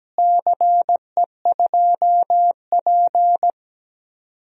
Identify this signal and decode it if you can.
Morse code, heard as a keyed tone, about 17 wpm, 700 Hz. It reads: CE2P
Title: DXCC-100 Call Signs - 4 Characters - Repeated 3x - 17wpm